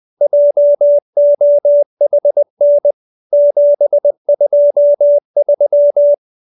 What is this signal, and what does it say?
Morse code, heard as a keyed tone, about 20 wpm, 580 Hz. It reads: JOHN 723